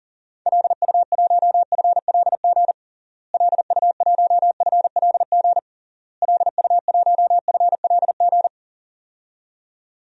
Morse code, heard as a keyed tone, about 38 wpm, 700 Hz. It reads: LU1FLG LU1FLG LU1FLG
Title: DXCC-100 Call Signs - 6 Characters - Repeated 3x - 40wpm